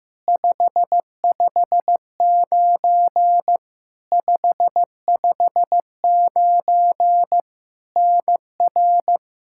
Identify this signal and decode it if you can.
Morse code, heard as a keyed tone, about 15 wpm, 700 Hz. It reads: 559 559 NR
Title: Practice QSO - Exchange 1 - Beginning - 15wpm